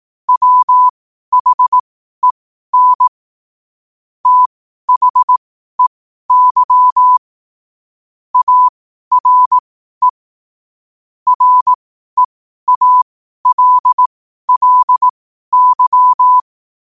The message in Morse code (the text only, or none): WHEN THEY ARE REALLY